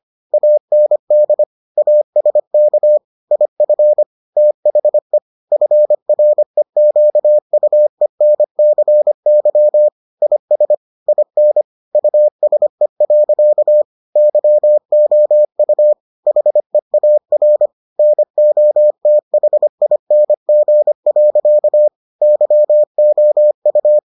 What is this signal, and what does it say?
Morse code, heard as a keyed tone, about 25 wpm, 600 Hz. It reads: AND ASK IF THE FREQUENCY IS IN USE. YOU HEAR NOTHING. YOU